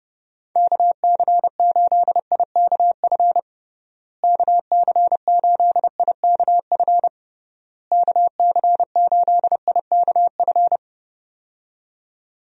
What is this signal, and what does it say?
Morse code, heard as a keyed tone, about 32 wpm, 700 Hz. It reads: KC8IKF KC8IKF KC8IKF
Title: USA Call Signs - 6 Characters - Repeated 3x - 30wpm